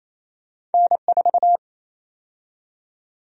Morse code, heard as a keyed tone, about 28 wpm, 700 Hz. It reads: N4
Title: USA Call Sign Prefixes - 28wpm